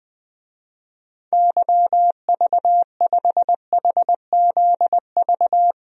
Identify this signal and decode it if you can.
Morse code, heard as a keyed tone, about 20 wpm, 700 Hz. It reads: YV5HZV
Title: DXCC-100 Call Signs - 6 Characters - 20wpm